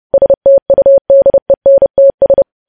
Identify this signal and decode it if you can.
Morse code, heard as a keyed tone, about 30 wpm, 550 Hz. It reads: STUDENTS
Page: news headlines in morse code at 30 wpm